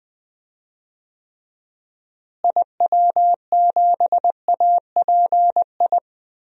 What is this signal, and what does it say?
Morse code, heard as a keyed tone, about 20 wpm, 700 Hz. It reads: IW7API
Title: DXCC-100 Call Signs - 6 Characters - Repeated 3x - 20wpm